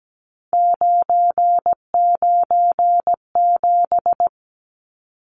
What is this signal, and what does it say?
Morse code, heard as a keyed tone, about 17 wpm, 700 Hz. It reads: 997
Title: Numbers - 3-Digit 17wpm